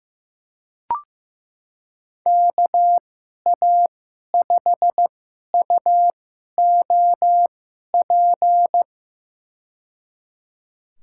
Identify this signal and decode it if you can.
Morse code, heard as a keyed tone, about 15 wpm, spaced as for 11 wpm, 700 Hz. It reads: KA5UOP
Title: USA Call Signs - 6 Characters - 2x Character Spacing - 15wpm